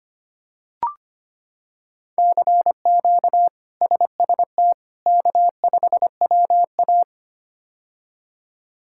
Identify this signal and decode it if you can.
Morse code, heard as a keyed tone, about 25 wpm, 700 Hz. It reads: CQ SST K5WA
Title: K1USN SST QSO Practice - 25wpm